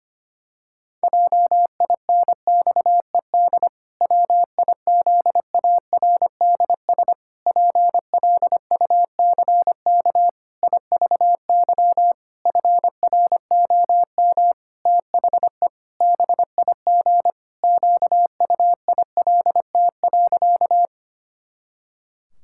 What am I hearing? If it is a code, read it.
Morse code, heard as a keyed tone, about 25 wpm, 700 Hz. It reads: JINXED WIZARDS PLUCK IVY FROM THE BIG QUILT.
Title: Pangram Challenge v2 - 25wpm